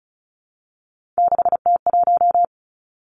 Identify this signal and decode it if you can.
Morse code, heard as a keyed tone, about 36 wpm, 700 Hz. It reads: 6T1
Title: Three Challenging Letter-Number Combinations 35wpm